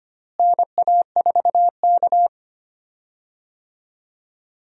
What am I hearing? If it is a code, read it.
Morse code, heard as a keyed tone, about 25 wpm, 700 Hz. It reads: NA4K
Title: USA Call Signs - 4 Characters - Repeated 2x - 25wpm